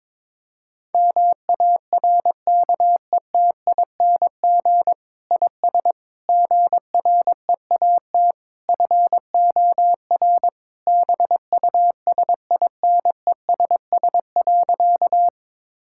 Morse code, heard as a keyed tone, about 22 wpm, 700 Hz. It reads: MARKETING IS GREAT FOR BUSINESS.
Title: Sentences from Top 1000 Words - Review Part 5 - 22wpm